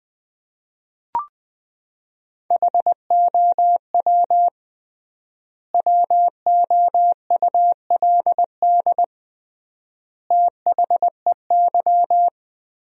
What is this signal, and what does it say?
Morse code, heard as a keyed tone, about 20 wpm, 700 Hz. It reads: HOW WOULD THEY
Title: 3x Word-Spacing - Sentences from Top 300 Words - 20wpm